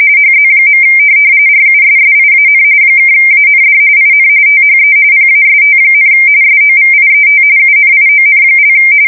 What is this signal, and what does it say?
RTTY, about 45.45 baud. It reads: HZ. RUSSIAN (AND IN THE PAST, SOVIET UNION)
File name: fsk170 / teletype-RTTY